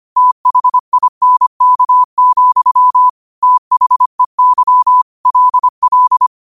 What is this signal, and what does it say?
Morse code, heard as a keyed tone, about 25 wpm, 1000 Hz. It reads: THINK, THEY LL
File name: qotdfiles / 25wpm